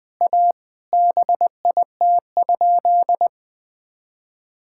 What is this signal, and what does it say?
Morse code, heard as a keyed tone, about 21 wpm, 700 Hz. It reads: A BIT?
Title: Sentences with 5-Word Phrases - Spoken Beginning - 20wpm